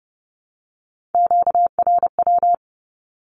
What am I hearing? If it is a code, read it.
Morse code, heard as a keyed tone, about 30 wpm, 700 Hz. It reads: QRW
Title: Q-Codes 30wpm